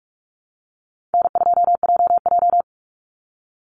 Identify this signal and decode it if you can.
Morse code, heard as a keyed tone, about 44 wpm, 700 Hz. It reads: N2JJ
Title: Easy Call Signs Repeated 3x - 45wpm